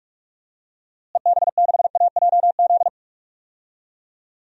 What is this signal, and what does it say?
Morse code, heard as a keyed tone, about 43 wpm, 700 Hz. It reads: EB6AJZ
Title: DXCC-100 Call Signs - 6 Characters - Repeated 3x - 45wpm